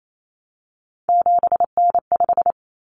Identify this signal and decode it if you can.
Morse code, heard as a keyed tone, about 28 wpm, 700 Hz. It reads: 7N5
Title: Common Cut Numbers - 3-Digit 28wpm